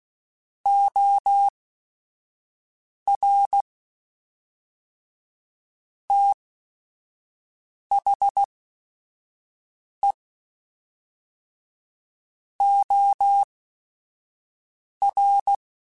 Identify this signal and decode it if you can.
Morse code, heard as a keyed tone, about 16 wpm, spaced as for 5 wpm, 800 Hz. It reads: OR THE OR
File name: Technical Subjects / Technical Subjects-mp3s / AND Gate @5